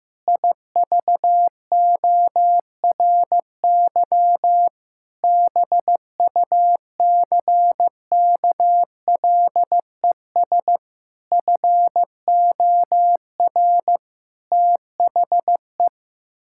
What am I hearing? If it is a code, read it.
Morse code, heard as a keyed tone, about 15 wpm, 700 Hz. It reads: IVORY BUCKLES FOR THE
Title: Pangram Challenge v2 - 15wpm